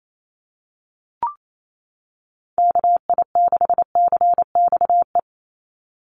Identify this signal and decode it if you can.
Morse code, heard as a keyed tone, about 28 wpm, 700 Hz. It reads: KI6CXE
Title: USA Call Signs 28wpm